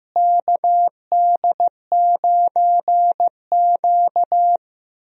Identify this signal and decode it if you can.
Morse code, heard as a keyed tone, about 15 wpm, 700 Hz. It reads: KD9Q
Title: USA Call Signs - 4 Characters - Repeated 3x - 15wpm